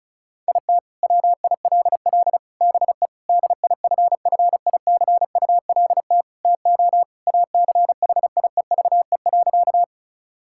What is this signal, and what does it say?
Morse code, heard as a keyed tone, about 34 wpm, 700 Hz. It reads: IT WILL BE DIFFICULT TO ACHIEVE.
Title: Sentences from Top 1000 Words - Review Part 6 - 35wpm